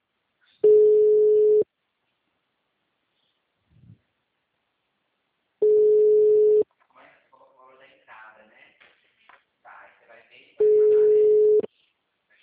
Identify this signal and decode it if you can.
Ringback tone (Europe, 425 Hz), with speech.